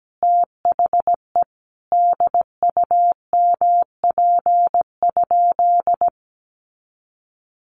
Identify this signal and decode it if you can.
Morse code, heard as a keyed tone, about 17 wpm, 700 Hz. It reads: THE DUMP?
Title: Sentences from Top 1300 Words 17wpm